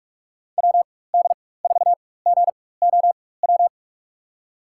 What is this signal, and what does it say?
Morse code, heard as a keyed tone, about 43 wpm, spaced as for 21 wpm, 700 Hz. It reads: WD4GOW
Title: USA Call Signs - 6 Characters - 4x Character Spacing - 45wpm